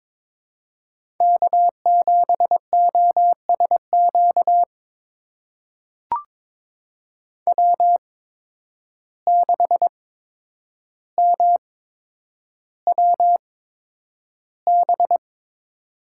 Morse code, heard as a keyed tone, about 22 wpm, spaced as for 9 wpm, 700 Hz. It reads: K7OSQ W6MWB
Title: USA Call Signs - 5 Characters - 8x Character Spacing - 22wpm